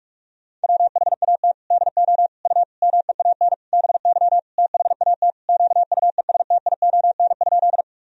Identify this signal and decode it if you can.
Morse code, heard as a keyed tone, about 41 wpm, 700 Hz. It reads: WHAT DO U MEAN BY THAT QUESTION?
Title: Sentences from Top 400 Words and Abbreviations 45wpm